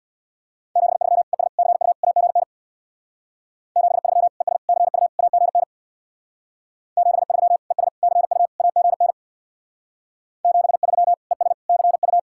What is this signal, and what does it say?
Morse code, heard as a keyed tone, about 60 wpm, 700 Hz. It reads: 73 ES CU AGN 73 ES CU AGN 73 ES CU AGN 73 ES CU
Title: Speed-Racing Common QSO Phrases - 50wpm